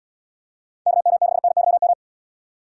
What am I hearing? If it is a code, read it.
Morse code, heard as a keyed tone, about 78 wpm, 700 Hz. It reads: KM6NYM